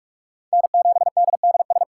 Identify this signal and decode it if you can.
Morse code, heard as a keyed tone, about 44 wpm, 700 Hz. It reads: N7DDS